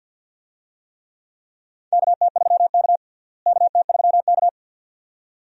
Morse code, heard as a keyed tone, about 53 wpm, 700 Hz. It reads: KT3K KT3K
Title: USA Call Signs - 4 Characters - Repeated 2x - 50wpm